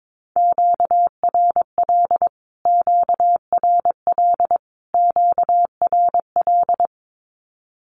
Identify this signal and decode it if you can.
Morse code, heard as a keyed tone, about 22 wpm, 700 Hz. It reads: QRL QRL QRL
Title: Q-Codes Repeated 3x 22wpm